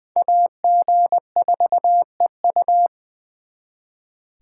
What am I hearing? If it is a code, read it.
Morse code, heard as a keyed tone, about 24 wpm, 700 Hz. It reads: AG4EU
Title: Speed Racing - USA 2x2 Call Signs 20wpm